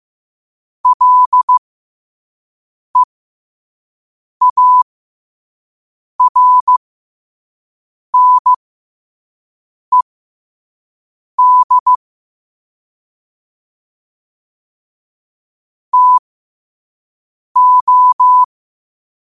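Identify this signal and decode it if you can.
Morse code, heard as a keyed tone, about 15 wpm, spaced as for 5 wpm, 1000 Hz. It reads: LEARNED TO
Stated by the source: Morse code quotes sent at 5 words per minute.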